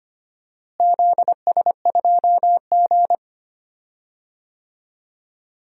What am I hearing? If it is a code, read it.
Morse code, heard as a keyed tone, about 25 wpm, 700 Hz. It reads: ZS2G
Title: DXCC-100 Call Signs - 4 Characters - Repeated 2x - 25wpm